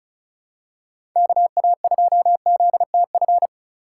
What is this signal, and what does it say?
Morse code, heard as a keyed tone, about 36 wpm, 700 Hz. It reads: KA2ZTF
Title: USA Call Signs - 6 Characters - 3x Character Spacing - 35wpm